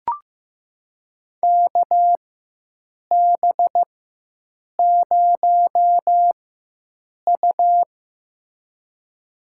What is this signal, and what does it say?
Morse code, heard as a keyed tone, about 15 wpm, spaced as for 7 wpm, 700 Hz. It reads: KB0U
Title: USA Call Signs - 4 Characters - 4x Character Spacing - 15wpm